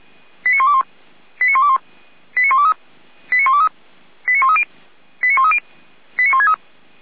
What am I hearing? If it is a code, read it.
Five-tone selective call (ZVEI), heard as quick runs of five tones, tones of about 80 ms. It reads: 89121 89121 89123 89123 89130 89130 89163
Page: Server at audio.f85360.feuerwehr-orlamuende.org